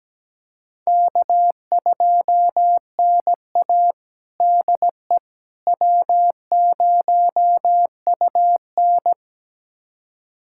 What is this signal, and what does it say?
Morse code, heard as a keyed tone, about 17 wpm, 700 Hz. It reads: K2NA DE W0UN
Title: USA Callsign DE Callsign - 4 Characters - 17wpm